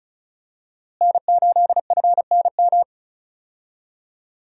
Morse code, heard as a keyed tone, about 36 wpm, 700 Hz. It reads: N8FNM
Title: USA Call Signs - 5 Characters - Repeated 3x - 35wpm